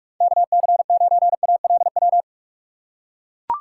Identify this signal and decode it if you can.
Morse code, heard as a keyed tone, about 44 wpm, 700 Hz. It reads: KC9ALW